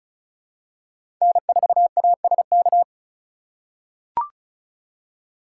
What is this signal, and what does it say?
Morse code, heard as a keyed tone, about 36 wpm, 700 Hz. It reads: N4ASK
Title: USA Call Signs - 5 Characters - 3x Character Spacing - 35wpm